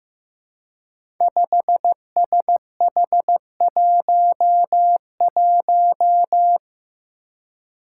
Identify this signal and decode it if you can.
Morse code, heard as a keyed tone, about 15 wpm, 700 Hz. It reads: 5SH11
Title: Five Challenging Letter-Number Combinations 15wpm